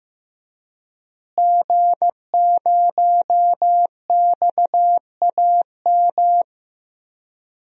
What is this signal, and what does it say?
Morse code, heard as a keyed tone, about 15 wpm, 700 Hz. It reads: G0XAM
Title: DXCC-100 Call Signs - 5 Characters - Repeated 3x - 15wpm